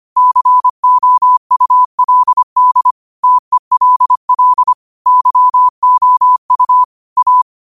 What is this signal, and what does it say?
Morse code, heard as a keyed tone, about 25 wpm, 1000 Hz. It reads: COULD TELL YOU A